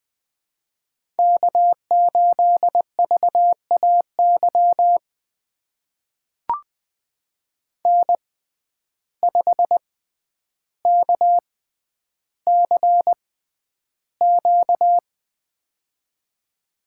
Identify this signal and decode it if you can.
Morse code, heard as a keyed tone, about 20 wpm, spaced as for 10 wpm, 700 Hz. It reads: K8VAY N5KCQ
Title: USA Call Signs - 5 Characters - 6x Character Spacing - 20wpm